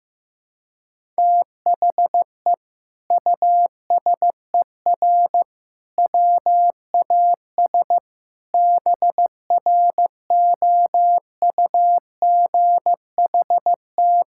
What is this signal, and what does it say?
Morse code, heard as a keyed tone, about 15 wpm, 700 Hz. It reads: THE USER WAS BROUGHT
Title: Sentences from Top 1000 Words - Review Part 5 - 15wpm